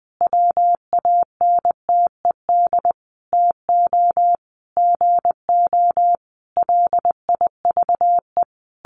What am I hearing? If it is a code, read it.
Morse code, heard as a keyed tone, about 20 wpm, 700 Hz. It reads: WANTED TO GO LIVE
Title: A Sound Of Thunder 20wpm